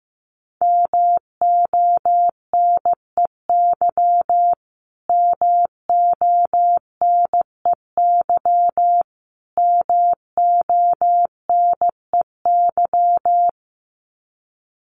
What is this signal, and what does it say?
Morse code, heard as a keyed tone, about 15 wpm, 700 Hz. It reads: MONEY MONEY MONEY
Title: Top 500 Words Repeated 15wpm